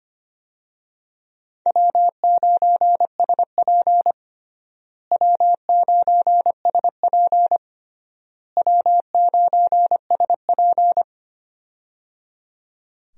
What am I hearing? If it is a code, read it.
Morse code, heard as a keyed tone, about 25 wpm, 700 Hz. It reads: W9SP W9SP W9SP
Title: USA Call Signs - 4 Characters - Repeated 3x - 25wpm